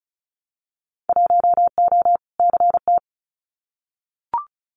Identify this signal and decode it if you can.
Morse code, heard as a keyed tone, about 36 wpm, 700 Hz. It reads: 1O CT
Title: Winter Field Day Exchanges 35wpm